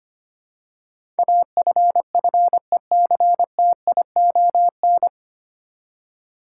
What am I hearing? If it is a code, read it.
Morse code, heard as a keyed tone, about 25 wpm, 700 Hz. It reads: AFFECTION
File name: practice / 25wpm / 9 Letter Words - First 3 Letters Given - 25wpm